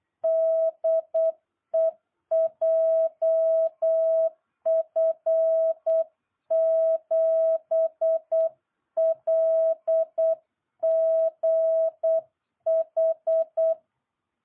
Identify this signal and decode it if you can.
Morse code, heard as a keyed tone, about 8 wpm, 650 Hz. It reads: DE JF7LGH